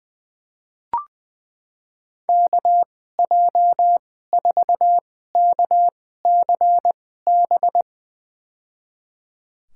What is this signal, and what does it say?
Morse code, heard as a keyed tone, about 20 wpm, spaced as for 15 wpm, 700 Hz. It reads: KJ4KCB
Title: USA Call Signs - 6 Characters - 2x Character Spacing - 20wpm